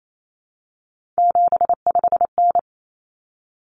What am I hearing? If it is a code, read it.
Morse code, heard as a keyed tone, about 28 wpm, 700 Hz. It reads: 75N
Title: Common Cut Numbers - 3-Digit 28wpm